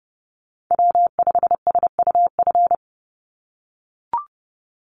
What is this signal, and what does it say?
Morse code, heard as a keyed tone, about 30 wpm, 700 Hz. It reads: W5SUF
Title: USA Call Signs 30wpm